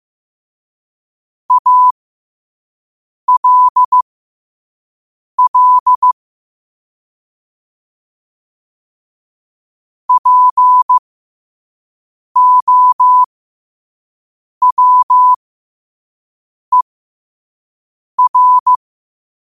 Morse code, heard as a keyed tone, about 15 wpm, spaced as for 5 wpm, 1000 Hz. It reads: ALL POWER
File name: 5wpm